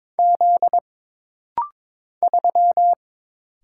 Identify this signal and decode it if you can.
Morse code, heard as a keyed tone, about 22 wpm, 700 Hz. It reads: Z 3
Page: This practice set is composed of Letters, Numbers, common punctuations, and common prosigns.